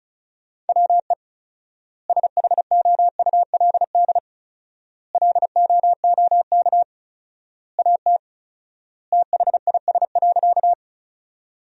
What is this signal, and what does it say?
Morse code, heard as a keyed tone, about 36 wpm, 700 Hz. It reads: WE SHOULD LOOK AT THIS.
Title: 4x Word-Spacing - Sentences from Top 300 Words - 35wpm